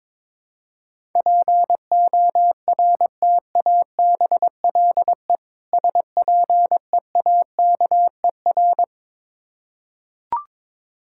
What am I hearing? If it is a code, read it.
Morse code, heard as a keyed tone, about 21 wpm, 700 Hz. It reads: PORTABLE SPEAKER
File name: Difficult Words with Hints - 22wpm